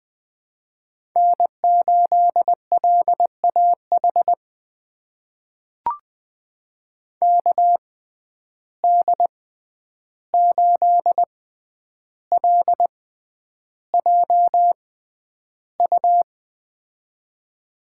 Morse code, heard as a keyed tone, about 20 wpm, spaced as for 10 wpm, 700 Hz. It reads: N8LAH KD8LJU
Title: USA Call Signs - 6x Character Spacing - 20wpm